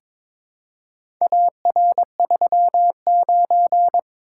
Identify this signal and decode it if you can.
Morse code, heard as a keyed tone, about 22 wpm, 700 Hz. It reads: AR39
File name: practice / 22wpm / Four Letter-Number Combinations - 22wpm